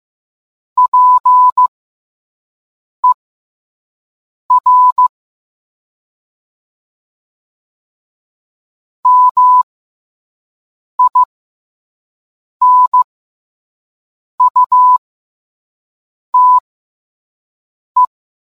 Morse code, heard as a keyed tone, about 15 wpm, spaced as for 5 wpm, 1000 Hz. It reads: PER MINUTE